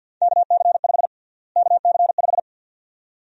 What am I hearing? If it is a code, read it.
Morse code, heard as a keyed tone, about 49 wpm, 700 Hz. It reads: KC5 KC5
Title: USA Call Sign Prefixes - Repeated 2x - 50wpm